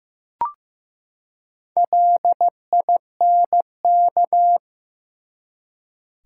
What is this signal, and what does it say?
Morse code, heard as a keyed tone, about 15 wpm, 700 Hz. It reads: LINK
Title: Four Letter Words - 15wpm